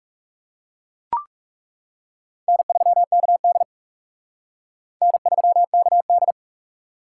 Morse code, heard as a keyed tone, about 36 wpm, 700 Hz. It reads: N3KD N3KD
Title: Speed Racing - USA 1x2 Call Signs 30wpm